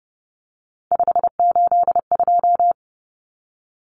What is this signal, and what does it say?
Morse code, heard as a keyed tone, about 30 wpm, 700 Hz. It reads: 582
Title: Numbers - 3-Digit 30wpm